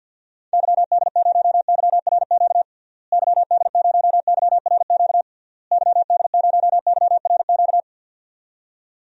Morse code, heard as a keyed tone, about 50 wpm, 700 Hz. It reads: YD0YRQ YD0YRQ YD0YRQ
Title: DXCC-100 Call Signs - 6 Characters - Repeated 3x - 50wpm